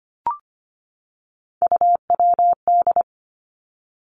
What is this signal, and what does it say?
Morse code, heard as a keyed tone, about 25 wpm, 700 Hz. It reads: UWD
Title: Three Challenging Letter-Number Combinations 25wpm